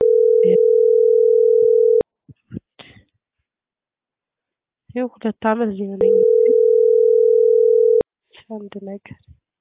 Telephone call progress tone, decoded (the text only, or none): ringback tone